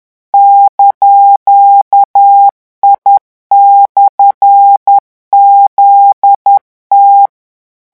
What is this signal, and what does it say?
Morse code, heard as a keyed tone, about 10 wpm, 800 Hz. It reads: )I/ZT
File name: morse mp3 / revision19-vitesse 10 mots